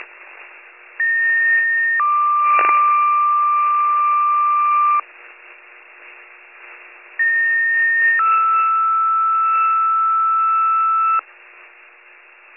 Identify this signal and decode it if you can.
Two-tone page, heard as a one-second tone, then a three-second tone, about 1850 then 1180 Hz, about 1850 then 1320 Hz. It is heard from 1.0 to 11.2 s.